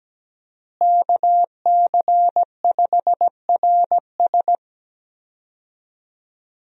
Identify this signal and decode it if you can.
Morse code, heard as a keyed tone, about 17 wpm, 700 Hz. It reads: KC5RS
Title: USA Call Signs - 5 Characters - Repeated 2x - 17wpm